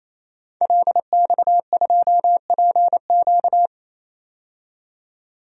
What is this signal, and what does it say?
Morse code, heard as a keyed tone, about 28 wpm, 700 Hz. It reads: LX2PQ
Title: DXCC-100 Call Signs - 5 Characters - 28wpm